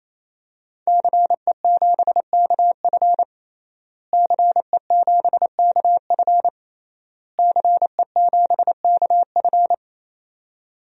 Morse code, heard as a keyed tone, about 28 wpm, 700 Hz. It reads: CE7KF CE7KF CE7KF
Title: DXCC-100 Call Signs - 5 Characters - Repeated 3x - 28wpm